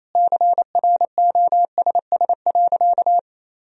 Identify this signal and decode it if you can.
Morse code, heard as a keyed tone, about 28 wpm, 700 Hz. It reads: CROSS.
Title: Sentences from Top 1000 Words - Review Part 1 - 28wpm